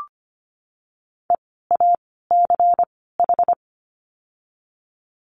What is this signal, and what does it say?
Morse code, heard as a keyed tone, about 25 wpm, spaced as for 16 wpm, 700 Hz. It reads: EACH
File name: course / 25:10wpm / 094 Lesson 16f - Letter c - Words 25wpm